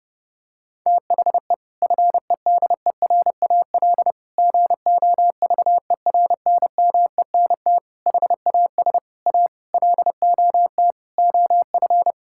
Morse code, heard as a keyed tone, about 32 wpm, 700 Hz. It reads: THE FEDERAL GOVERNMENT HAS A LOT OF
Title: Sentences from Top 1000 Words - Review Part 5 - 30wpm